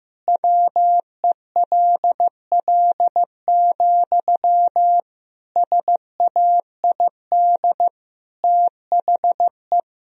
Morse code, heard as a keyed tone, about 15 wpm, 700 Hz. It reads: WELL, SAID THE
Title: Sentences from Top 1000 Words - Review Part 4 - 15wpm